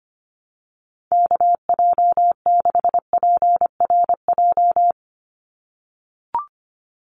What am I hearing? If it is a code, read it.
Morse code, heard as a keyed tone, about 25 wpm, 700 Hz. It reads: KJ6PRJ
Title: USA Call Signs 25wpm